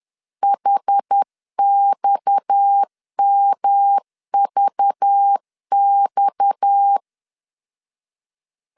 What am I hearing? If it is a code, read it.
Morse code, heard as a keyed tone, about 10 wpm, 800 Hz. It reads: HXMVX
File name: LEZIONI / 6 PKX0 / 6 PKX 10wpm 2426